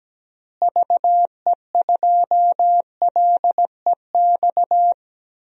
Morse code, heard as a keyed tone, about 17 wpm, 700 Hz. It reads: VE2LEX
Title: DXCC-100 Call Signs - 6 Characters - Repeated 2x - 17wpm